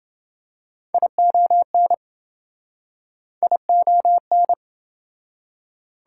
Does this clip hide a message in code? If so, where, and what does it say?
Yes: Morse code from 0.9 to 4.5 s, heard as a keyed tone, about 27 wpm, 700 Hz. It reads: ION ION